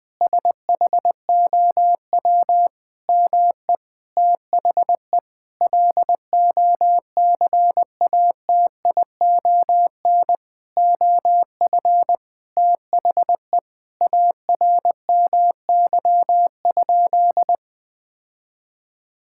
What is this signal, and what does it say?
Morse code, heard as a keyed tone, about 20 wpm, 700 Hz. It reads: SHOW ME THE LOCATION OF THE ARMY?
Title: Sentences from Top 1000 Words - Review Part 6 - 20wpm